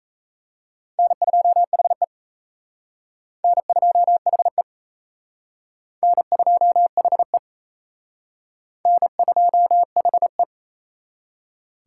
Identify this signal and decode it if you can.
Morse code, heard as a keyed tone, about 34 wpm, 700 Hz. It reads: N2HE N2HE N2HE N2HE
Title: Speed Racing - USA 1x2 Call Signs 28wpm